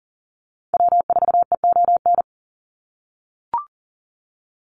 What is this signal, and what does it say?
Morse code, heard as a keyed tone, about 40 wpm, 700 Hz. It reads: W4EON
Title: NA SOTA Activator Callsigns 40wpm